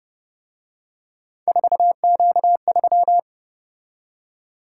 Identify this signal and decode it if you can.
Morse code, heard as a keyed tone, about 30 wpm, 700 Hz. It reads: 4Q3